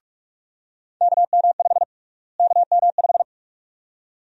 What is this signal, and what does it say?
Morse code, heard as a keyed tone, about 46 wpm, 700 Hz. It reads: KM5 KM5
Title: USA Call Sign Prefixes - Repeated 2x - 45wpm